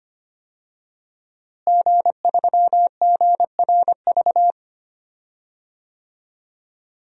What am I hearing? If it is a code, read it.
Morse code, heard as a keyed tone, about 25 wpm, 700 Hz. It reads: G3GRV
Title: DXCC-100 Call Signs - 5 Characters - Repeated 3x - 25wpm